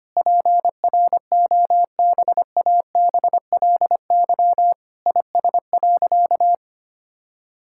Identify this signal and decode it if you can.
Morse code, heard as a keyed tone, about 25 wpm, 700 Hz. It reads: PROBABLY IS.
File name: practice / 25wpm / Sentences with 9-Word Phrases - 25wpm